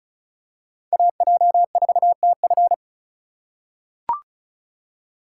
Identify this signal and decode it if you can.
Morse code, heard as a keyed tone, about 36 wpm, 700 Hz. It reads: AJ4TF
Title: K1USN SST QSO Practice - 35wpm